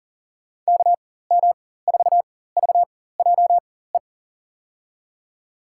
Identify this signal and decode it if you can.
Morse code, heard as a keyed tone, about 40 wpm, spaced as for 19 wpm, 700 Hz. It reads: KM4VJE
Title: USA Call Signs - 4x Character Spacing - 40wpm